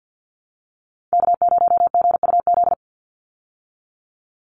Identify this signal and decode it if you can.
Morse code, heard as a keyed tone, about 50 wpm, 700 Hz. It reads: K0GUZ
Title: Call Signs Repeated 3x - 50wpm